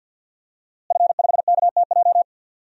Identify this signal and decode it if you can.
Morse code, heard as a keyed tone, about 50 wpm, 700 Hz. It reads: F5KTJ
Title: DXCC-100 Call Signs - 5 Characters - Repeated 3x - 50wpm